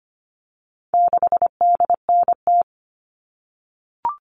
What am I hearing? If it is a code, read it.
Morse code, heard as a keyed tone, about 25 wpm, 700 Hz. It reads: 6DNT
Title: All Cut Numbers - 4-Digit 25wpm